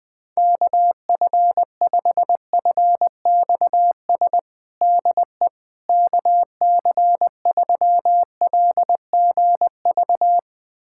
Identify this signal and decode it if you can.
Morse code, heard as a keyed tone, about 21 wpm, 700 Hz. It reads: KF5FXS DE KC3LGV
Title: USA Callsign DE Callsign - 6 Characters - 20wpm